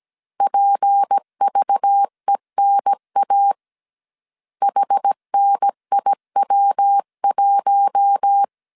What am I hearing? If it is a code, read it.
Morse code, heard as a keyed tone, about 17 wpm, 800 Hz. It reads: PVENA HNIW1
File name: 16wpm